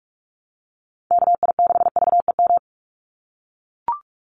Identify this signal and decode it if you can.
Morse code, heard as a keyed tone, about 44 wpm, 700 Hz. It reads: KI6VEM
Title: USA Call Signs 45wpm